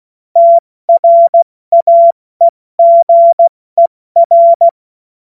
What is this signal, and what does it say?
Morse code, heard as a keyed tone, about 16 wpm, spaced as for 14 wpm, 680 Hz. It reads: TRAEGER